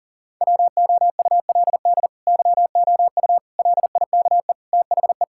Morse code, heard as a keyed tone, about 40 wpm, 700 Hz. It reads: WOULD YOU LIKE THE
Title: Sentences from Top 1600 Words 40wpm